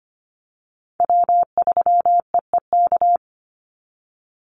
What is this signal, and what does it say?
Morse code, heard as a keyed tone, about 25 wpm, 700 Hz. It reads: W3EEK
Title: Call Signs - 25wpm